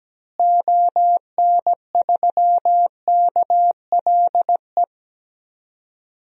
Morse code, heard as a keyed tone, about 17 wpm, 700 Hz. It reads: ON3KLE
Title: DXCC-100 Call Signs - 6 Characters - Repeated 2x - 17wpm